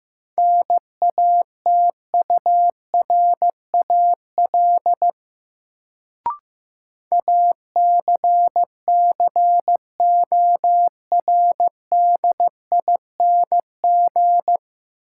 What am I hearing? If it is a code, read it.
Morse code, heard as a keyed tone, about 15 wpm, 700 Hz. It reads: NATURAL ACCORDING
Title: Sentences from Top 700 Words - New Word Drill - 15wpm